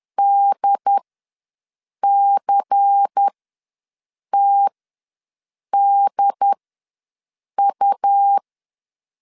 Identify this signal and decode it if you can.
Morse code, heard as a keyed tone, about 10 wpm, spaced as for 6 wpm, 800 Hz. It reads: DCTDU
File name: LEZIONI / 5 LBVC / 5 LBVC ws 10wpm 3733